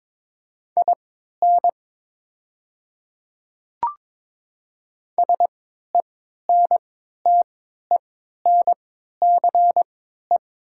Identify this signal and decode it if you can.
Morse code, heard as a keyed tone, about 22 wpm, spaced as for 12 wpm, 700 Hz. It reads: IN SENTENCE
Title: 3x Character Spacing - Top 200 Words - 22wpm